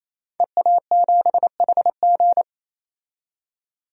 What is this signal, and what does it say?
Morse code, heard as a keyed tone, about 28 wpm, 700 Hz. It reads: EA7HG
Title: DXCC-100 Call Signs - 5 Characters - 28wpm